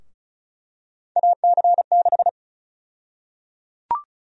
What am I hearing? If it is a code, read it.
Morse code, heard as a keyed tone, about 36 wpm, 700 Hz. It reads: AC6
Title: USA Call Sign Prefixes - Repeated 2x - 35wpm